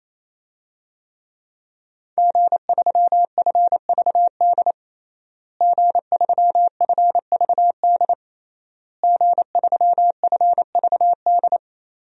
Morse code, heard as a keyed tone, about 27 wpm, 700 Hz. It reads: G3FVD G3FVD G3FVD
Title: DXCC-100 Call Signs - 5 Characters - Repeated 3x - 28wpm